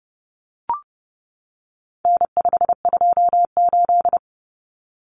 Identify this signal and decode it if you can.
Morse code, heard as a keyed tone, about 30 wpm, 700 Hz. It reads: N528
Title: Common Cut Numbers - 4-Digit 30wpm